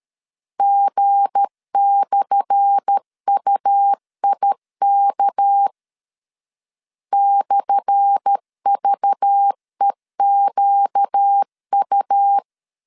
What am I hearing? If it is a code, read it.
Morse code, heard as a keyed tone, about 13 wpm, 800 Hz. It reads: G/UIK /VEQU
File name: LEZIONI / 10 SIGNS / 10A SLASH 12wpm 2314